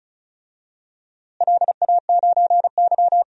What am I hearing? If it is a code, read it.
Morse code, heard as a keyed tone, about 36 wpm, 700 Hz. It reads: LA9Y
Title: DXCC-100 Call Signs - 4 Characters - Repeated 3x - 35wpm